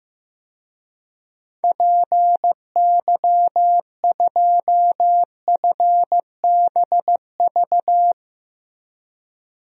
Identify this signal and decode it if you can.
Morse code, heard as a keyed tone, about 15 wpm, 700 Hz. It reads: PY2FBV
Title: DXCC-100 Call Signs - 6 Characters - 15wpm